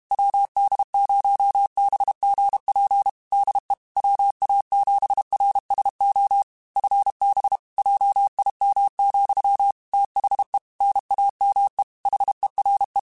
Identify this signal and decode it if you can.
Morse code, heard as a keyed tone, about 32 wpm, 800 Hz. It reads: WD0BGP DE WA7RSO FB JIM, THE NAME HERE
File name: QSOs / QSOs-mp3s / WD0BGP 1@30